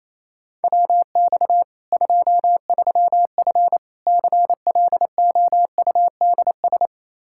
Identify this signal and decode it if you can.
Morse code, heard as a keyed tone, about 28 wpm, 700 Hz. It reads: WX 23F CLOUDS
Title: Practice QSO - Exchange 2 - Info - 28wpm